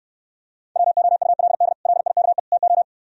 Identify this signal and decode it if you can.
Morse code, heard as a keyed tone, about 73 wpm, 700 Hz. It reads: WOULD LIKE TO